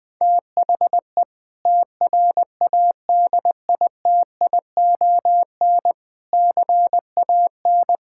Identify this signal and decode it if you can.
Morse code, heard as a keyed tone, about 21 wpm, 700 Hz. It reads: THE TRADITION CAN
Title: Sentences with 6-Word Phrases - Spoken Beginning - 20wpm